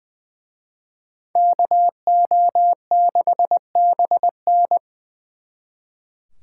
Morse code, heard as a keyed tone, about 23 wpm, 700 Hz. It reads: KO6BN
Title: Speed Racing - USA 2x2 Call Signs 20wpm